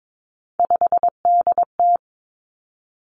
Morse code, heard as a keyed tone, about 22 wpm, 700 Hz. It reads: 5DT
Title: Three Challenging Letter-Number Combinations 22wpm